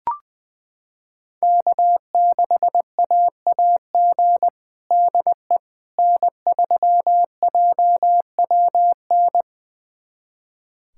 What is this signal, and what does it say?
Morse code, heard as a keyed tone, about 21 wpm, 700 Hz. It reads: K6AAG DE N3JWN
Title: USA Callsign DE Callsign - 5 Characters - 20wpm